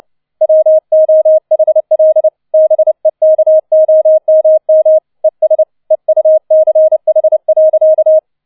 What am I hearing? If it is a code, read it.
Morse code, heard as a keyed tone, about 29 wpm, 630 Hz. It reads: WOHL BEKOMM ES EUCH.